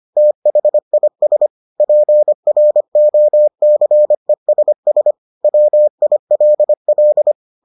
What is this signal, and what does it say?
Morse code, heard as a keyed tone, about 25 wpm, 600 Hz. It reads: THIS PROCESS WILL